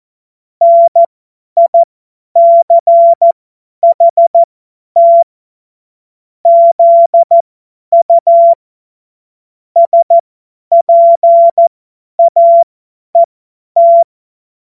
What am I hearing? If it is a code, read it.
Morse code, heard as a keyed tone, about 14 wpm, spaced as for 10 wpm, 680 Hz. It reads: NICHT ZU SPAET
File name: Morsekurs / Witze / Witz 14. Februar 2023 50 F70